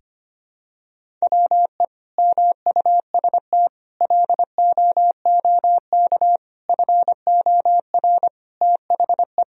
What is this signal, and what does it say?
Morse code, heard as a keyed tone, about 25 wpm, 700 Hz. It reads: WE MUST LOOK FOR THE
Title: Sentences from Top 1000 Words - Review Part 2 - 25wpm